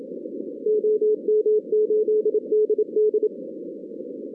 Morse code, heard as a keyed tone, about 29 wpm, 440 Hz. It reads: OM8DD
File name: 2016 arrl dx cw / 2016-ARRL-DX-CW-K5TR- - 0-41